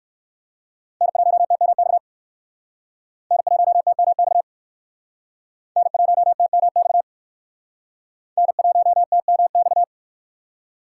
Morse code, heard as a keyed tone, about 59 wpm, 700 Hz. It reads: N1TMX N1TMX N1TMX N1TMX